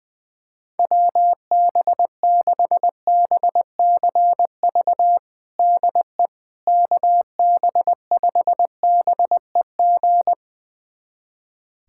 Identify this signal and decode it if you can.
Morse code, heard as a keyed tone, about 20 wpm, 700 Hz. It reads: WB6BCV DE KB5BEG
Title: USA Callsign DE Callsign - 6 Characters - 20wpm